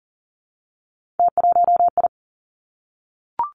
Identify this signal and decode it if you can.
Morse code, heard as a keyed tone, about 38 wpm, 700 Hz. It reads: T1I
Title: Three Challenging Letter-Number Combinations 40wpm